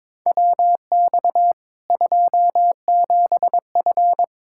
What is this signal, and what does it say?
Morse code, heard as a keyed tone, about 22 wpm, 700 Hz. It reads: WX 27F
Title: Practice QSO - Exchange 2 - Info - 22wpm